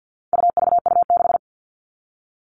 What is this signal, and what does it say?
Morse code, heard as a keyed tone, about 50 wpm, 700 Hz. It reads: UVU6
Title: All Cut Numbers - 4-Digit 50wpm